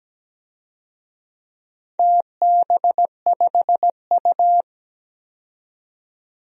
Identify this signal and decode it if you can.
Morse code, heard as a keyed tone, about 17 wpm, 700 Hz. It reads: TB5U
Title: DXCC-100 Call Signs - 4 Characters - 17wpm